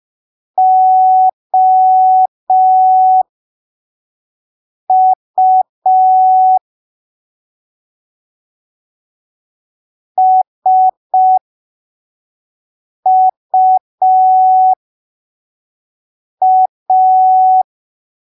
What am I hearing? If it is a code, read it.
Morse code, heard as a keyed tone, about 5 wpm, spaced as for 3 wpm, 750 Hz. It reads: OU SUA